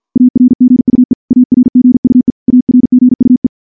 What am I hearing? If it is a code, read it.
Morse code, heard as a keyed tone, about 50 wpm, 250 Hz. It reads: ARGUE ARGUE ARGUE
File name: morse / morse-training-50WPM